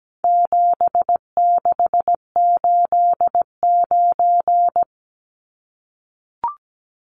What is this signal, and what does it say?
Morse code, heard as a keyed tone, about 17 wpm, 700 Hz. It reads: 7689
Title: Numbers - 4-Digit 17wpm